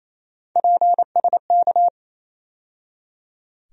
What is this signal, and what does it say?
Morse code, heard as a keyed tone, about 28 wpm, 700 Hz. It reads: PSK
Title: USA Call Sign Postfixes - 3 Letters 28wpm